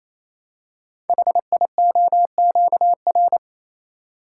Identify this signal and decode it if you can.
Morse code, heard as a keyed tone, about 27 wpm, 700 Hz. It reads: HIOQR
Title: Code Groups - 5 Letter:Numbers - 28wpm